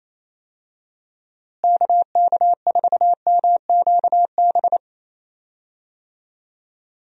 Morse code, heard as a keyed tone, about 28 wpm, 700 Hz. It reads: KK4MQB
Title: USA Call Signs - 6 Characters - Repeated 2x - 28wpm